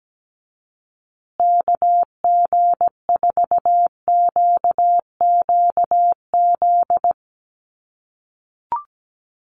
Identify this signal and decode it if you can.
Morse code, heard as a keyed tone, about 17 wpm, 700 Hz. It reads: KG4QQZ
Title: USA Call Signs Repeated 2x 17wpm